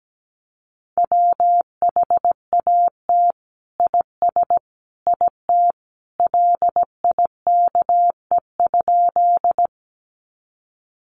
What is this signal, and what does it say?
Morse code, heard as a keyed tone, about 17 wpm, 700 Hz. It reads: WHAT IS IT LIKE?
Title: Sentences from Top 100 Words 17wpm